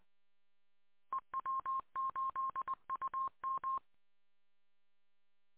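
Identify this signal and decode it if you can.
Morse code, heard as a keyed tone, about 20 wpm, 1060 Hz. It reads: E W8UM
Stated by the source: Morse code recorded in the W8UM audio archive, 145.230.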